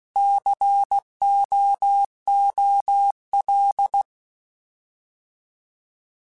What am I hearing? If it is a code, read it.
Morse code, heard as a keyed tone, about 16 wpm, 800 Hz. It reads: COOL
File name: Alphabetical / Alphabetical mp3s / Cs @15